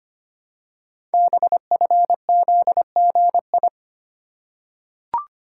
Morse code, heard as a keyed tone, about 25 wpm, 700 Hz. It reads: BFZGI
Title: Code Groups - 5 Letters - 25wpm